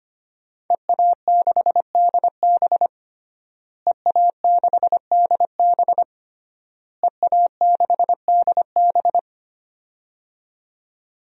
Morse code, heard as a keyed tone, about 25 wpm, 700 Hz. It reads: EA6DB EA6DB EA6DB
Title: DXCC-100 Call Signs - 5 Characters - Repeated 3x - 25wpm